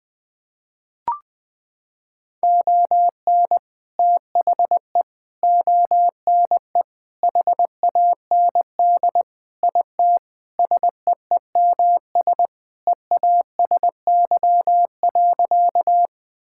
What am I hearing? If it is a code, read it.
Morse code, heard as a keyed tone, about 21 wpm, 700 Hz. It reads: ON THE ONE HAND IT SEEMS EASY.
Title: Sentences with 4-Word Phrases - 20wpm